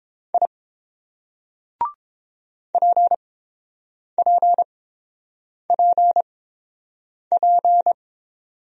Morse code, heard as a keyed tone, about 29 wpm, spaced as for 7 wpm, 690 Hz. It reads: I PPPP